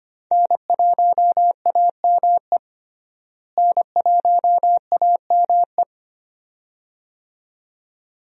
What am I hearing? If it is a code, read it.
Morse code, heard as a keyed tone, about 25 wpm, 700 Hz. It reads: N1AME N1AME
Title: USA Call Signs - 5 Characters - Repeated 2x - 25wpm